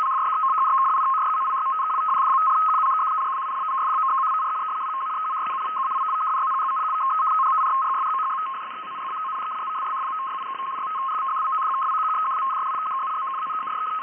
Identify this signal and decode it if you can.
Two-tone teleprinter signal, about 200 Hz shift.
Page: ALGERIAN AIR FORCE 75Bd / 300 Bd / 200 Hz BELL 103 COMPATIBLE FSK MODEM